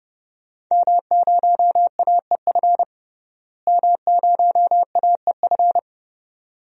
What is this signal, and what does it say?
Morse code, heard as a keyed tone, about 32 wpm, 700 Hz. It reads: M0AEF M0AEF
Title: DXCC-100 Call Signs - 5 Characters - Repeated 2x - 30wpm